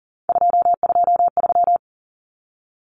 Morse code, heard as a keyed tone, about 40 wpm, 700 Hz. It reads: 223
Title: Common Cut Numbers - 3-Digit 40wpm